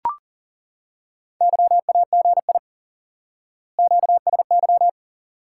Morse code, heard as a keyed tone, about 40 wpm, 700 Hz. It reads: YAGI QSY
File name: IQE Course 3 - Lesson 89 - YAGI - Two Elements - 40wpm